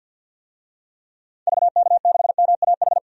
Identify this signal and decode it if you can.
Morse code, heard as a keyed tone, about 52 wpm, 700 Hz. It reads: VK6MAH
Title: DXCC-100 Call Signs - 6 Characters - 50wpm